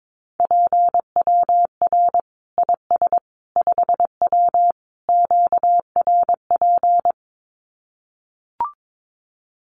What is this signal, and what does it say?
Morse code, heard as a keyed tone, about 22 wpm, 700 Hz. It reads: PWR IS 5W QRP
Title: Common QSO Phrases 22wpm